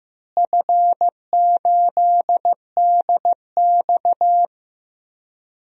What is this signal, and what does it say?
Morse code, heard as a keyed tone, about 15 wpm, 700 Hz. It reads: F8DX
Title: DXCC-100 Call Signs - 4 Characters - Repeated 2x - 15wpm